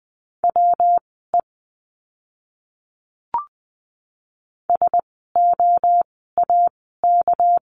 Morse code, heard as a keyed tone, about 20 wpm, spaced as for 14 wpm, 700 Hz. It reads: WE SOAK
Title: 199 Lesson 32f - Letter k - Words 20wpm